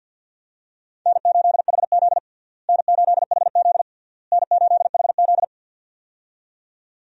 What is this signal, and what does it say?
Morse code, heard as a keyed tone, about 51 wpm, 700 Hz. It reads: N8HZ N8HZ N8HZ
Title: USA Call Signs - 4 Characters - Repeated 3x - 50wpm